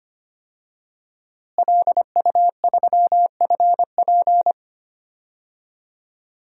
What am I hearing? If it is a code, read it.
Morse code, heard as a keyed tone, about 25 wpm, 700 Hz. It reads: LU3FP
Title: DXCC-100 Call Signs - 5 Characters - Repeated 2x - 25wpm